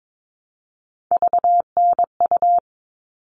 Morse code, heard as a keyed tone, about 22 wpm, 700 Hz. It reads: VNU
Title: All Cut Numbers - 3-Digit 22wpm